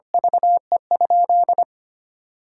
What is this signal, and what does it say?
Morse code, heard as a keyed tone, about 25 wpm, 700 Hz. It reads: VE?
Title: Sentences from Top 1000 Words - Review Part 6 - 25wpm